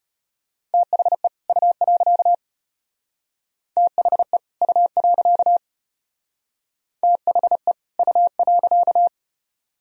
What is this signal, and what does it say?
Morse code, heard as a keyed tone, about 32 wpm, 700 Hz. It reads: THE U. THE U. THE U.